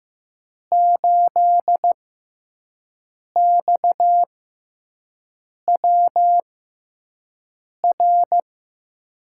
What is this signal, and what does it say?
Morse code, heard as a keyed tone, about 15 wpm, spaced as for 5 wpm, 700 Hz. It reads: 8XWR
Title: USA Call Signs - 5 Characters - 6x Character Spacing - 15wpm